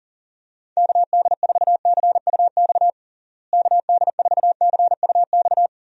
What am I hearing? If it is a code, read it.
Morse code, heard as a keyed tone, about 40 wpm, 700 Hz. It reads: KD4CUX KD4CUX
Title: USA Call Signs - 6 Characters - Repeated 2x - 40wpm